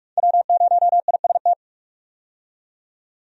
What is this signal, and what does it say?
Morse code, heard as a keyed tone, about 47 wpm, 700 Hz. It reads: W0IST
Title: USA Call Signs - 5 Characters - 45wpm